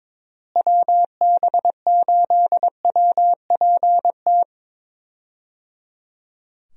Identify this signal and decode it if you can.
Morse code, heard as a keyed tone, about 21 wpm, 700 Hz. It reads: WB8WPT
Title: USA Call Signs - 6 Characters - Repeated 3x - 22wpm